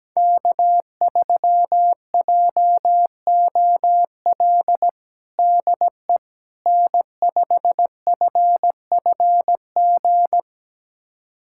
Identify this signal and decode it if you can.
Morse code, heard as a keyed tone, about 17 wpm, 700 Hz. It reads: K3JOL DE N5FFG
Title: USA Callsign DE Callsign - 5 Characters - 17wpm